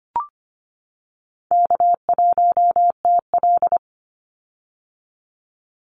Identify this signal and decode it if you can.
Morse code, heard as a keyed tone, about 25 wpm, 700 Hz. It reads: K1TL
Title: Easy Call Signs - 25wpm